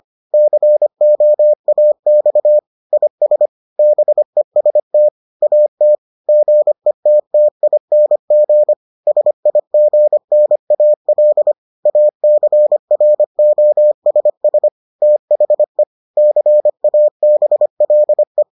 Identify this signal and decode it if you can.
Morse code, heard as a keyed tone, about 25 wpm, 600 Hz. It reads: COAX IS BEST AT GETTING SIGNAL ACROSS THE CABLE